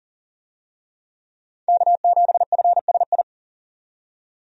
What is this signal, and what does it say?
Morse code, heard as a keyed tone, about 40 wpm, 700 Hz. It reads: K7FSI
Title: USA Call Signs - 5 Characters - 40wpm